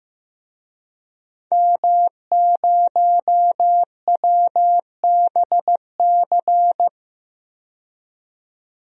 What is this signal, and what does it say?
Morse code, heard as a keyed tone, about 15 wpm, 700 Hz. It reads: M0WBC
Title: DXCC-100 Call Signs - 5 Characters - 15wpm